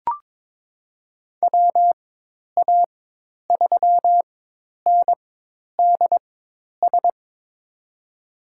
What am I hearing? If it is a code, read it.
Morse code, heard as a keyed tone, about 22 wpm, spaced as for 10 wpm, 700 Hz. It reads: WA3NDS
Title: USA Call Signs - 6 Characters - 4x Character Spacing - 22wpm